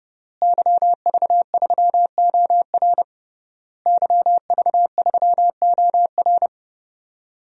Morse code, heard as a keyed tone, about 30 wpm, 700 Hz. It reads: YV3OR YV3OR
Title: DXCC-100 Call Signs - 5 Characters - Repeated 3x - 30wpm